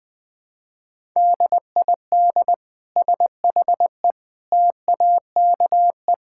Morse code, heard as a keyed tone, about 21 wpm, 700 Hz. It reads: DID SHE TAKE
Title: Sentences from Top 1000 Words - Review Part 6 - 20wpm